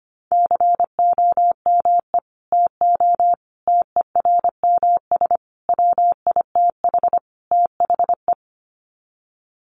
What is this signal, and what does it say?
Morse code, heard as a keyed tone, about 25 wpm, 700 Hz. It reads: COME TO TERMS WITH THE
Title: Sets of 5 Words Encore2 25wpm